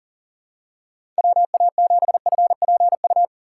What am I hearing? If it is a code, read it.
Morse code, heard as a keyed tone, about 40 wpm, 700 Hz. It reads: WA7FPU
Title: USA Call Signs - 6 Characters - 6x Character Spacing - 40wpm